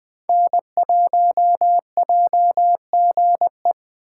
Morse code, heard as a keyed tone, about 21 wpm, 700 Hz. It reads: N1JGE